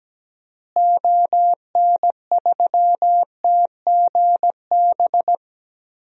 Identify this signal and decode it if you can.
Morse code, heard as a keyed tone, about 17 wpm, 700 Hz. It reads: ON3TGB
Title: DXCC-100 Call Signs - 6 Characters - Repeated 2x - 17wpm